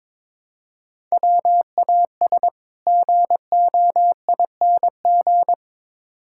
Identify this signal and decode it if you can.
Morse code, heard as a keyed tone, about 22 wpm, 700 Hz. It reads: WAS GOING
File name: practice / 22wpm / Build A Phrase - 5-Word Phrases - 22wpm